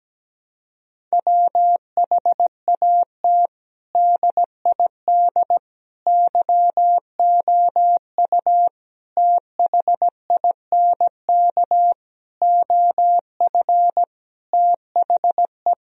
Morse code, heard as a keyed tone, about 17 wpm, 700 Hz. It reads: WHAT DID YOU THINK OF THE
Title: Sentences from Top 1000 Words - Review Part 4 - 17wpm